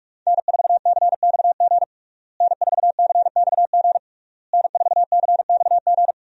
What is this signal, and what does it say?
Morse code, heard as a keyed tone, about 46 wpm, 700 Hz. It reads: N4CXG N4CXG N4CXG
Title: USA Call Signs - 5 Characters - Repeated 3x - 45wpm